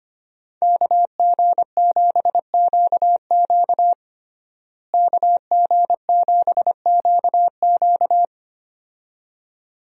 Morse code, heard as a keyed tone, about 25 wpm, 700 Hz. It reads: KG7QQ KG7QQ
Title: USA Call Signs - 5 Characters - Repeated 2x - 25wpm